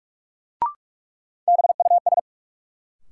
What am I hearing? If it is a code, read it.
Morse code, heard as a keyed tone, about 42 wpm, 700 Hz. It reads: BUS
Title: Sentences from Top 1000 Words - Review Part 6 - New Word Drill - 45wpm